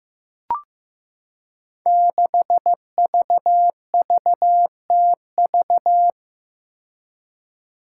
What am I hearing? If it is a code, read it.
Morse code, heard as a keyed tone, about 15 wpm, 700 Hz. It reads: 6VVTV
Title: Five Challenging Letter-Number Combinations 15wpm